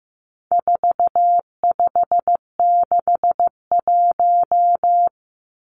Morse code, heard as a keyed tone, about 15 wpm, 700 Hz. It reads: 4561
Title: Numbers - 4-Digit 15wpm